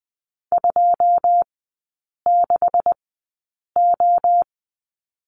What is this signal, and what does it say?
Morse code, heard as a keyed tone, about 20 wpm, spaced as for 8 wpm, 700 Hz. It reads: 26O
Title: 127 Lesson 21e - Number 6 - Three Characters 20wpm